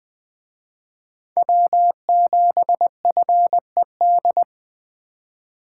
Morse code, heard as a keyed tone, about 23 wpm, 700 Hz. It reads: W7FED
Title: Speed Racing - USA 1x3 Call Signs 20wpm